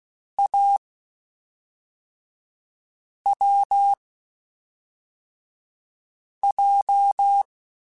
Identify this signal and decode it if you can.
Morse code, heard as a keyed tone, about 16 wpm, spaced as for 3 wpm, 800 Hz. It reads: AWJ
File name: Short Runs / Short Runs-mp3s / Unballanced @5